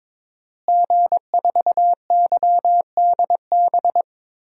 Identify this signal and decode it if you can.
Morse code, heard as a keyed tone, about 21 wpm, 700 Hz. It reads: G4YDB
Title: DXCC-100 Call Signs - 5 Characters - Repeated 2x - 22wpm